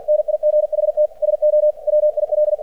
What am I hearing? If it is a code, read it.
Morse code, heard as a keyed tone, about 45 wpm, 600 Hz. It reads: NIGHT SO WIL
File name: archived / CRSnet